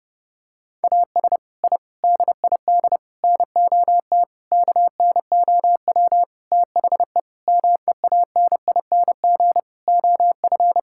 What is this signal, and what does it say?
Morse code, heard as a keyed tone, about 32 wpm, 700 Hz. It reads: AS I DID NOT KNOW THE MEANING OF
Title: Sentences from Top 1000 Words - Review Part 6 - 30wpm